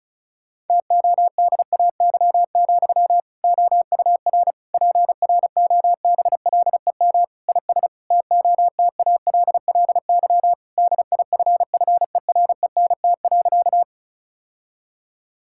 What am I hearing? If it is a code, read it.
Morse code, heard as a keyed tone, about 36 wpm, 700 Hz. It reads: TODAY, OUR PROBLEM IS TOTALLY DIFFERENT.
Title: Sentences from Top 1600 Words 35wpm